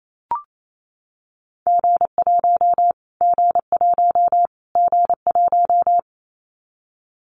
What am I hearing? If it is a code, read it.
Morse code, heard as a keyed tone, about 28 wpm, 700 Hz. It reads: G1 G1 G1
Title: Callsign Prefixes for Top 25 DXCC - Repeated 3x - 28wpm